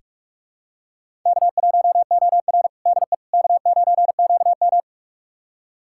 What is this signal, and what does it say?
Morse code, heard as a keyed tone, about 44 wpm, 690 Hz. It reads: K1OR DE K9QM
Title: USA Callsign DE Callsign - 4 Characters - 45wpm